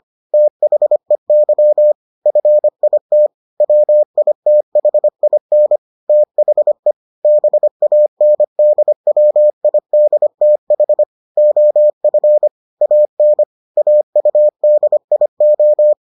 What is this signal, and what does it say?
Morse code, heard as a keyed tone, about 25 wpm, 600 Hz. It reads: THEY FIT WITHIN THE BANDWIDTH OF AN AUDIO